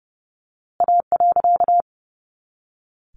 Morse code, heard as a keyed tone, about 29 wpm, 700 Hz. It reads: A.
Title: A Sound Of Thunder 30wpm